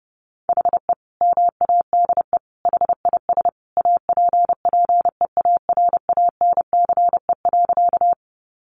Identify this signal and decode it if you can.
Morse code, heard as a keyed tone, about 30 wpm, 700 Hz. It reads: HE MADE HIS APPEARANCE.
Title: Sentences from Top 1100 Words 30wpm